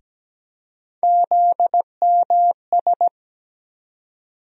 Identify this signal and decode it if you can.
Morse code, heard as a keyed tone, about 17 wpm, 700 Hz. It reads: ZMS
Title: USA Call Sign Postfixes - 3 Letters 17wpm